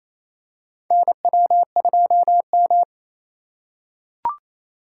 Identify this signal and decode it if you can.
Morse code, heard as a keyed tone, about 28 wpm, 700 Hz. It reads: NW2M
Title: USA Call Signs - 4 Characters - 3x Character Spacing - 28wpm